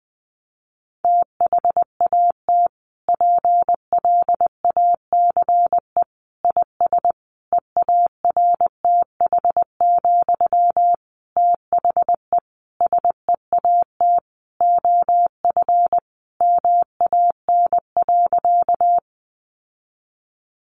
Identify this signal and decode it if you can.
Morse code, heard as a keyed tone, about 20 wpm, 700 Hz. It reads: THAT PLACE IS EARTH, THE SEAT OF MAN.
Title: Sentences from Top 1500 Words 20wpm